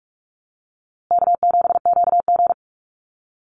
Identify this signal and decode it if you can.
Morse code, heard as a keyed tone, about 44 wpm, 700 Hz. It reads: K7QG
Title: USA Call Signs Repeated 3x 45wpm